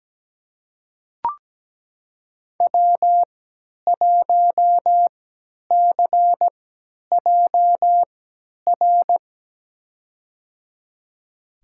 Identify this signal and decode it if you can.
Morse code, heard as a keyed tone, about 17 wpm, spaced as for 10 wpm, 700 Hz. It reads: W1CJR
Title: USA Call Signs - 5 Characters - 3x Character Spacing - 17wpm